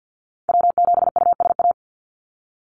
Morse code, heard as a keyed tone, about 50 wpm, 700 Hz. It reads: W7USA
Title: NA SOTA Activator Callsigns 50wpm